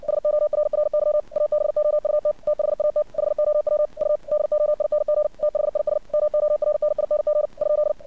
Morse code, heard as a keyed tone, about 57 wpm, 600 Hz. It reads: SORRY ABOUT THAT HOW U DOING THIS MORNING ?
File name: archived / CRSnet